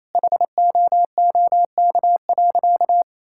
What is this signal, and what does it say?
Morse code, heard as a keyed tone, about 27 wpm, 700 Hz. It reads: HOOK.
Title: Sentences from Top 1600 Words 28wpm